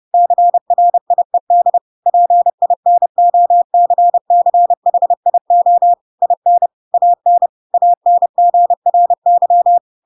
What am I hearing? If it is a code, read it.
Morse code, heard as a keyed tone, about 30 wpm, 700 Hz. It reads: CRIED PINOCCHIO IN AN ANGRY
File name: Pinocchio / Pinocchio ch7 30 30 ik1hge